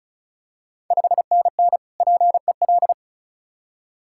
Morse code, heard as a keyed tone, about 36 wpm, 690 Hz. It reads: 5NN PEL
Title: Ontario QSO Party Exchange 35wpm